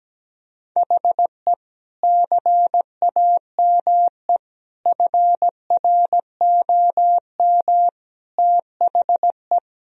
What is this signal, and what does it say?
Morse code, heard as a keyed tone, about 17 wpm, 700 Hz. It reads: HE CAME FROM THE
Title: Sentences from Top 1000 Words - Review Part 2 - 17wpm